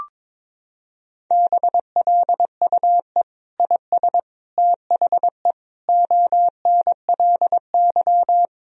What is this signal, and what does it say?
Morse code, heard as a keyed tone, about 22 wpm, 700 Hz. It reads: BLUE IS THE ONLY
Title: Sentences from Top 1000 Words - Review Part 3 - 22wpm